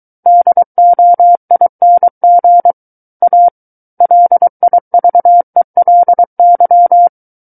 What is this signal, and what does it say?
Morse code, heard as a keyed tone, about 23 wpm, 700 Hz. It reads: DOING A LIVELY
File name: Book One / 14-Chapter 14 - 23 WPM